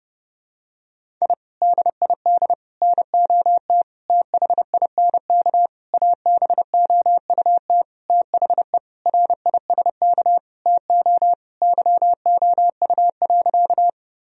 Morse code, heard as a keyed tone, about 30 wpm, 700 Hz. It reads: I DID NOT THINK ABOUT THE RISK TO YOU.
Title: Sentences from Top 1000 Words - Review Part 4 - 30wpm